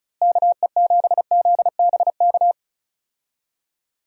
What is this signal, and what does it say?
Morse code, heard as a keyed tone, about 34 wpm, 700 Hz. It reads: KE7ZBK
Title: USA Call Signs - 6 Characters - Repeated 3x - 35wpm